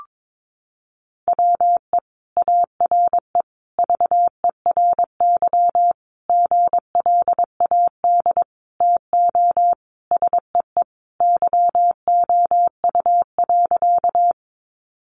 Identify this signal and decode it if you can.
Morse code, heard as a keyed tone, about 22 wpm, 700 Hz. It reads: WE ARE VERY GLAD TO SEE YOU.
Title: Sentences from Top 800 Words 22wpm